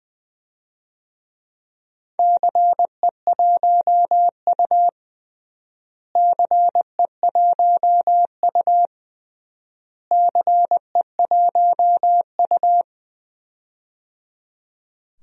Morse code, heard as a keyed tone, about 20 wpm, 700 Hz. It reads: CE1U CE1U CE1U
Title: DXCC-100 Call Signs - 4 Characters - Repeated 3x - 20wpm